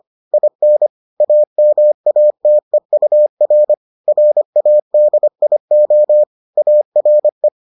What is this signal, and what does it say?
Morse code, heard as a keyed tone, about 25 wpm, 600 Hz. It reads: IN AMATEUR RADIO ARE